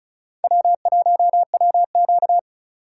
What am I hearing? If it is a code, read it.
Morse code, heard as a keyed tone, about 44 wpm, 700 Hz. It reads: W1WQ